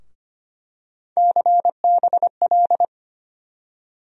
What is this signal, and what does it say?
Morse code, heard as a keyed tone, about 25 wpm, 700 Hz. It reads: CBL
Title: USA Call Sign Postfixes - 3 Letters 25wpm